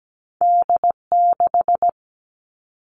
Morse code, heard as a keyed tone, about 17 wpm, 700 Hz. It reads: D 6
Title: Two Challenging Letter-Number Combinations 17wpm